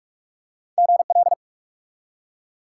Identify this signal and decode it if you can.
Morse code, heard as a keyed tone, about 46 wpm, 700 Hz. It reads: GL
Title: Top 90 QSO Elements 45wpm